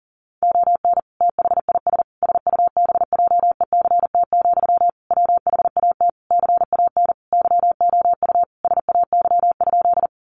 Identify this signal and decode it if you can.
Morse code, heard as a keyed tone, about 40 wpm, 700 Hz. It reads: ON THIS SUBJECT, WHAT CAN YOU SAY?
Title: Sentences from Top 800 Words 40wpm